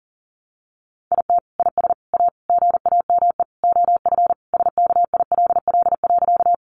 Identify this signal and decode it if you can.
Morse code, heard as a keyed tone, about 40 wpm, 700 Hz. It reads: IT IS A GAME OF SKILL.
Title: Sentences from Top 1200 Words 40wpm